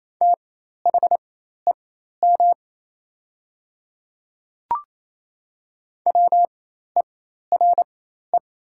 Morse code, heard as a keyed tone, about 28 wpm, spaced as for 13 wpm, 700 Hz. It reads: THEM WERE
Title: 4x Character Spacing - Top 100 Words - 28wpm